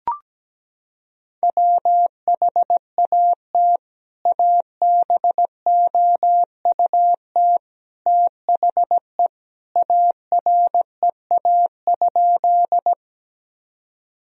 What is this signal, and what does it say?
Morse code, heard as a keyed tone, about 17 wpm, 700 Hz. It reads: WHAT ABOUT THE AREA?
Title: Sentences from Top 1000 Words - Review Part 2 - 17wpm